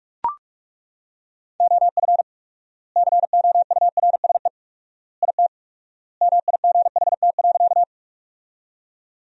Morse code, heard as a keyed tone, about 42 wpm, 700 Hz. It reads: OF COURSE IT MIGHT.
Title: 4x Word-Spacing - Sentences from Top 500 Words - 45wpm